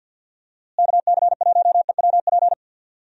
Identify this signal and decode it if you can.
Morse code, heard as a keyed tone, about 51 wpm, 700 Hz. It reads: KC1EWP
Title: USA Call Signs - 6 Characters - Repeated 2x - 50wpm